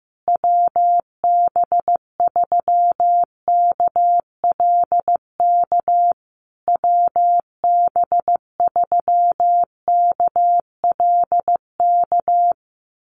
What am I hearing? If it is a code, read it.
Morse code, heard as a keyed tone, about 15 wpm, 700 Hz. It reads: WB3KLK WB3KLK
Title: USA Call Signs Repeated 3x 15wpm